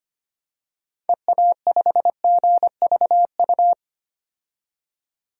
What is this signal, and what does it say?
Morse code, heard as a keyed tone, about 25 wpm, 700 Hz. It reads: EA5GVU
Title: DXCC-100 Call Signs - 6 Characters - 25wpm